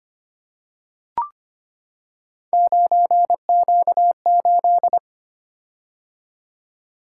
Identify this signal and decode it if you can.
Morse code, heard as a keyed tone, about 25 wpm, 700 Hz. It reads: 9Q8
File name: Three Letter-Number Combinations - 25wpm